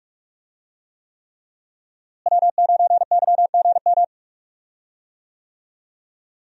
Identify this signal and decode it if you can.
Morse code, heard as a keyed tone, about 45 wpm, 700 Hz. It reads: W9YGM
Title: USA Call Signs - 5 Characters - Repeated 3x - 45wpm